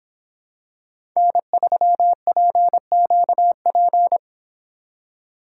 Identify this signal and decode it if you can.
Morse code, heard as a keyed tone, about 26 wpm, 700 Hz. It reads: N3PQP